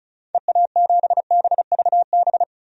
Morse code, heard as a keyed tone, about 36 wpm, 700 Hz. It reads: EA7BVB
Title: DXCC-100 Call Signs - 6 Characters - Repeated 2x - 35wpm